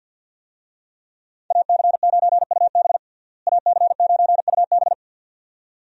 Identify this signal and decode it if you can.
Morse code, heard as a keyed tone, about 50 wpm, 700 Hz. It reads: AC9UB AC9UB
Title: USA Call Signs - 5 Characters - Repeated 2x - 50wpm